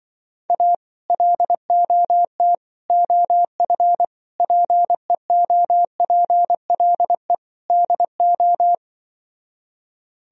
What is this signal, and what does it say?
Morse code, heard as a keyed tone, about 23 wpm, 700 Hz. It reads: A LOT OF PEOPLE DO
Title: Speed-Racing - Sets of 5 Words 20wpm